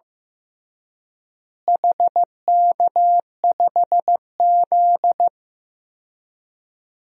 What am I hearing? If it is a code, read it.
Morse code, heard as a keyed tone, about 15 wpm, 700 Hz. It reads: HK5Z
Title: DXCC-100 Call Signs - 4 Characters - Repeated 2x - 15wpm